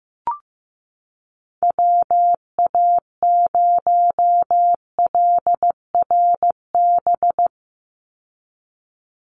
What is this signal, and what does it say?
Morse code, heard as a keyed tone, about 15 wpm, 700 Hz. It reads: WA0LRB
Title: USA Call Signs 15wpm